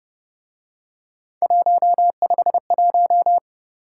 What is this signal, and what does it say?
Morse code, heard as a keyed tone, about 29 wpm, 700 Hz. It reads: 151